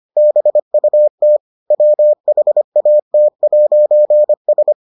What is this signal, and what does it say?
Morse code, heard as a keyed tone, about 25 wpm, 600 Hz. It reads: BUT WHAT'S